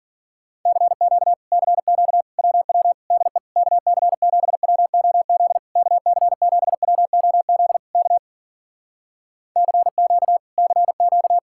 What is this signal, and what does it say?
Morse code, heard as a keyed tone, about 53 wpm, 700 Hz. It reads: CQ CQ WW DE KC7WOZ KC7WOZ K CQ CQ